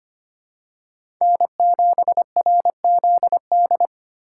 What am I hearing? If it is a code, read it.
Morse code, heard as a keyed tone, about 25 wpm, 700 Hz. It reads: N7RZD
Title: USA Call Signs - 5 Characters - 4x Character Spacing - 25wpm